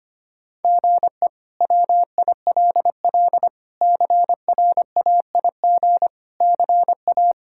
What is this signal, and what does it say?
Morse code, heard as a keyed tone, about 25 wpm, 700 Hz. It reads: GE WILL CRAIG CA
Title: K1USN SST QSO Practice - 25wpm